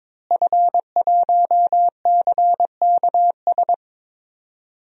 Morse code, heard as a keyed tone, about 22 wpm, 700 Hz. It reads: F1CKS
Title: DXCC-100 Call Signs - 5 Characters - Repeated 3x - 22wpm